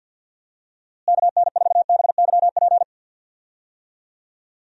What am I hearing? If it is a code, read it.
Morse code, heard as a keyed tone, about 50 wpm, 700 Hz. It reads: KN4BYP
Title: USA Call Signs - 6 Characters - Repeated 2x - 50wpm